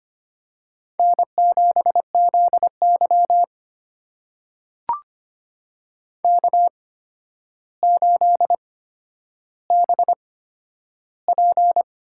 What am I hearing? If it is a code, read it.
Morse code, heard as a keyed tone, about 25 wpm, spaced as for 11 wpm, 700 Hz. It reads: N7ZY K8BP
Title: USA Call Signs - 8x Character Spacing - 25wpm